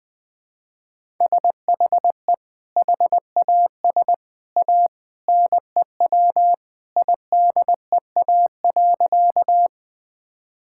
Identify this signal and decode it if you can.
Morse code, heard as a keyed tone, about 20 wpm, 700 Hz. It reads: SHE HAS A NEW IDEA.
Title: Sentences with 2-Word Phrases - 20wpm